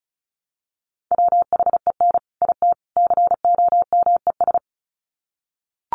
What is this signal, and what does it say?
Morse code, heard as a keyed tone, about 34 wpm, 700 Hz. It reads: WHEN IT COMES
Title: Sets of 3 Words 35wpm